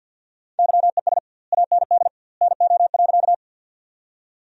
Morse code, heard as a keyed tone, about 50 wpm, 700 Hz. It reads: YES AND NO.
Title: 2x Word-Spacing - Sentences from Top 500 Words - 50wpm